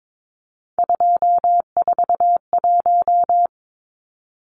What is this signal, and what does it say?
Morse code, heard as a keyed tone, about 22 wpm, 700 Hz. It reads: 241
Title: Numbers - 3-Digit 22wpm